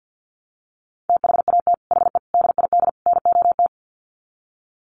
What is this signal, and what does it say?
Morse code, heard as a keyed tone, about 50 wpm, 700 Hz. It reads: THAT HE DID NOT
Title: Sets of 4 Words 50wpm